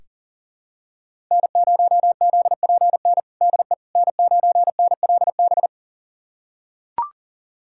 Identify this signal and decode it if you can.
Morse code, heard as a keyed tone, about 40 wpm, 700 Hz. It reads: N0ZPN DE N9NLB
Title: USA Callsign DE Callsign 40wpm